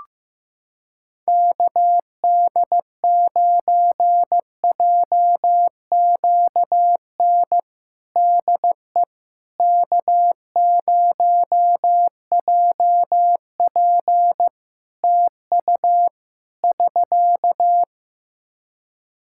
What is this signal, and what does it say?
Morse code, heard as a keyed tone, about 16 wpm, 700 Hz. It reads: KD9JQN DE K0JP TU <SK>
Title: Practice QSO - Exchange 4 - Ending - 15wpm